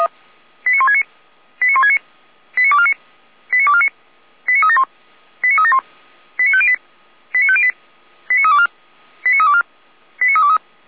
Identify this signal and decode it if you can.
Five-tone selective call (ZVEI), heard as quick runs of five tones, tones of about 70 ms. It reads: 89169 89169 89249 89249 89371 89371 89508 89508 89324 89324 89323
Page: /455/SDR/ZVEI/89167/